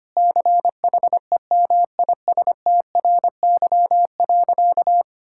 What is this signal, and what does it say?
Morse code, heard as a keyed tone, about 25 wpm, 700 Hz. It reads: CHEMISTRY.
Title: Sentences from Top 1000 Words - Review Part 6 - 25wpm